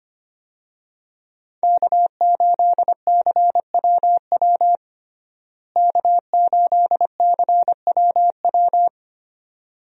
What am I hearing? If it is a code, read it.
Morse code, heard as a keyed tone, about 25 wpm, 700 Hz. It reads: K8CWW K8CWW
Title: USA Call Signs - 5 Characters - Repeated 2x - 25wpm